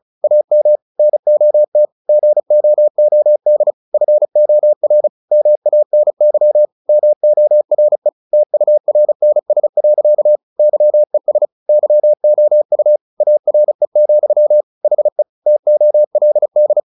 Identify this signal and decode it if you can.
Morse code, heard as a keyed tone, about 34 wpm, 600 Hz. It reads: AM NOT GOOD FOR MANY MORE TURNS. YES YOU ARE, HE TOLD
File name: the old man and the sea 2 35wpm